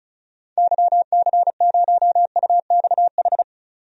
Morse code, heard as a keyed tone, about 36 wpm, 690 Hz. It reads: YC0UXH
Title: DXCC-100 Call Signs - 6 Characters - 35wpm